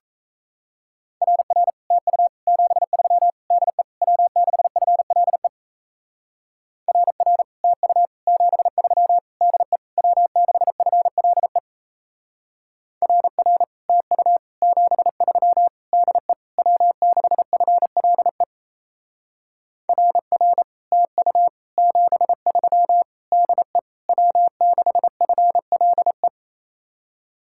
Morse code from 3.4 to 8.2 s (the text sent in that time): DE W6FLE RR TU